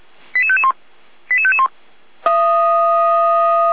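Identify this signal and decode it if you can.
Five-tone selective call (ZVEI), heard as quick runs of five tones, tones of about 70 ms. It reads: 88551 88551
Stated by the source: ZVEI five-tone call received by SDR, hosted by feuerwehr-orlamuende (fire brigade Orlamünde).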